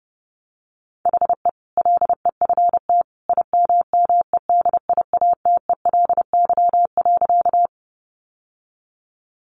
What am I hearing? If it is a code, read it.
Morse code, heard as a keyed tone, about 30 wpm, 700 Hz. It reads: HE LEFT IMMEDIATELY.
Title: Sentences from Top 1100 Words 30wpm